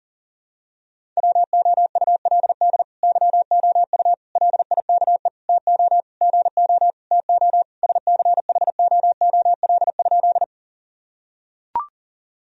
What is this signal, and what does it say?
Morse code, heard as a keyed tone, about 37 wpm, 700 Hz. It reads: WOULD YOU LIKE TO GO TO SCHOOL?
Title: Sentences from Top 1000 Words - Review Part 1 - 40wpm